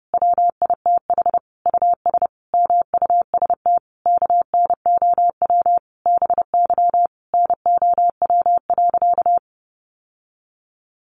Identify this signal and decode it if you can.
Morse code, heard as a keyed tone, about 30 wpm, 700 Hz. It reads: WITH US MUST KNOW BY NOW.
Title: Sentences from Top 1400 Words 30wpm